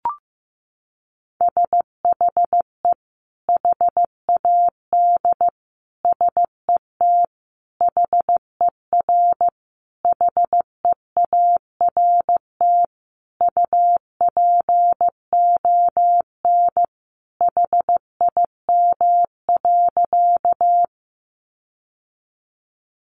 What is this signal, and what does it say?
Morse code, heard as a keyed tone, about 15 wpm, 700 Hz. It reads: SHE HAD SET HER HEART UPON HIM.
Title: Sentences from Top 1200 Words 15wpm